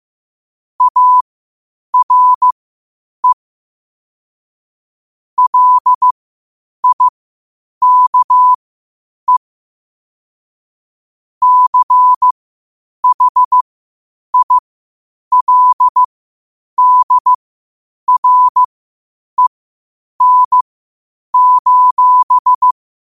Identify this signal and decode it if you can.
Morse code, heard as a keyed tone, about 15 wpm, spaced as for 8 wpm, 1000 Hz. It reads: ARE LIKE CHILDREN:
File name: qotdfiles / 8wpm